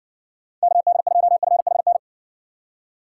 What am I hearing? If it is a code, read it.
Morse code, heard as a keyed tone, about 68 wpm, 700 Hz. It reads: KD2FHN